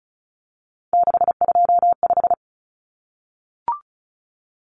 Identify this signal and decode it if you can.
Morse code, heard as a keyed tone, about 34 wpm, 700 Hz. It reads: 625
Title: Numbers - 3-Digit 35wpm